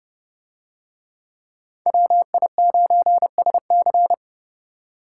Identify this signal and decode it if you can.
Morse code, heard as a keyed tone, about 28 wpm, 700 Hz. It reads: WI9SC